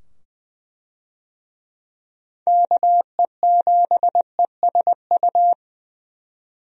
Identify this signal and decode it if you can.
Morse code, heard as a keyed tone, about 20 wpm, 700 Hz. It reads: KE7ESU
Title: USA Call Signs - 6 Characters - 20wpm